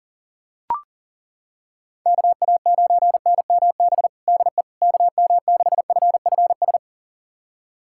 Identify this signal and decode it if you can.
Morse code, heard as a keyed tone, about 40 wpm, 700 Hz. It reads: KA9NMB DE KM6FFS
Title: USA Callsign DE Callsign 40wpm